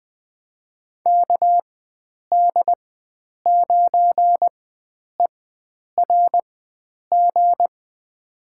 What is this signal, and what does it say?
Morse code, heard as a keyed tone, about 20 wpm, spaced as for 9 wpm, 700 Hz. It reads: KD9ERG
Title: USA Call Signs - 6 Characters - 4x Character Spacing - 20wpm